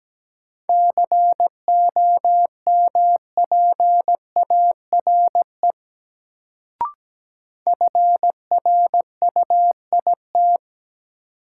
Morse code, heard as a keyed tone, about 17 wpm, 700 Hz. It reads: COMPARE FRUIT
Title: Sentences from Top 1300 Words - New Word Drill Alt - 17wpm